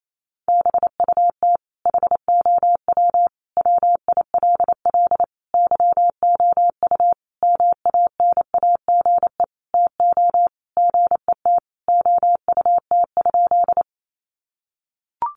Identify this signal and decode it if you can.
Morse code, heard as a keyed tone, about 28 wpm, 700 Hz. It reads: BUT HOW WILL YOU MANAGE TO GET OUT?
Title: Sentences from Top 1400 Words 28wpm